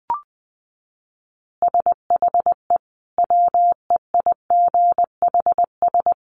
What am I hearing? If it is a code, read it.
Morse code, heard as a keyed tone, about 20 wpm, 700 Hz. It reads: SHE WEIGHS
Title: Easy Phrases 20wpm